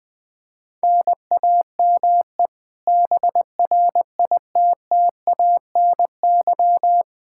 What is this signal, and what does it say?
Morse code, heard as a keyed tone, about 20 wpm, 700 Hz. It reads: NAME BRITTANY
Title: K1USN SST QSO Practice 20wpm